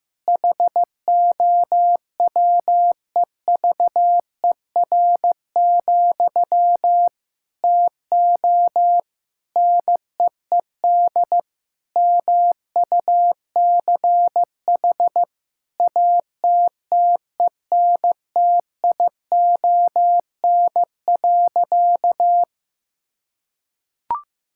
Morse code, heard as a keyed tone, about 16 wpm, 700 Hz. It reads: HOWEVER, TO NEED MUCH ATTENTION.
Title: Sentences from Top 1000 Words - Review Part 6 - 15wpm